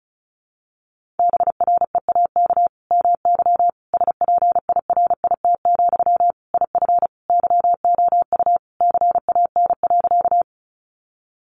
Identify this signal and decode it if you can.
Morse code, heard as a keyed tone, about 34 wpm, 700 Hz. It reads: BREAK MY SPIRIT, IF YOU CAN.
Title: Sentences from Top 1400 Words 35wpm